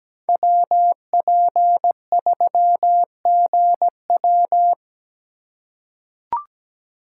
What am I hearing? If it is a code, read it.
Morse code, heard as a keyed tone, about 17 wpm, 700 Hz. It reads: WP3GW
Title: USA Call Signs - 5 Characters - 4x Character Spacing - 17wpm